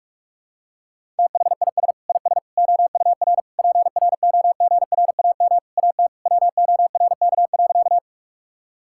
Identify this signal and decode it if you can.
Morse code, heard as a keyed tone, about 46 wpm, 700 Hz. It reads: THIS IS OUR PROGRAM AT WORK.
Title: Sentences from Top 1000 Words - Review Part 5 - 45wpm